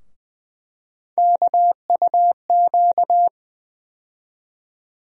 Morse code, heard as a keyed tone, about 21 wpm, 700 Hz. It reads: KUQ
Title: USA Call Sign Postfixes - 3 Letters 20wpm